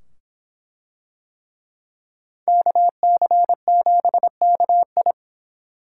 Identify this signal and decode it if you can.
Morse code, heard as a keyed tone, about 24 wpm, 700 Hz. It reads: KC7KI